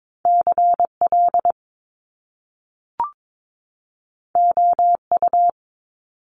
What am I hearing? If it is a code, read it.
Morse code, heard as a keyed tone, about 22 wpm, 700 Hz. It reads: CL OU
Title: Top Two Letter Combinations in Words 22wpm